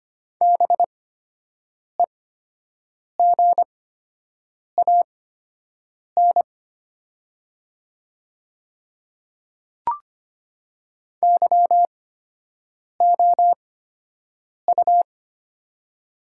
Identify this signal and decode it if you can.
Morse code, heard as a keyed tone, about 25 wpm, spaced as for 7 wpm, 700 Hz. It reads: BEGAN YOU
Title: 8x Character Spacing - Top 500 Words - 25wpm